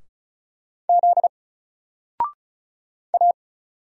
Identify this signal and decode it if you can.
Morse code, heard as a keyed tone, about 36 wpm, 700 Hz. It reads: Z A
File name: Single-Letter-Number-35wpm